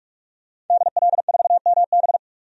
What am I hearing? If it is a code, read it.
Morse code, heard as a keyed tone, about 46 wpm, 700 Hz. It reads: DL4MB
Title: DXCC-100 Call Signs - 5 Characters - 45wpm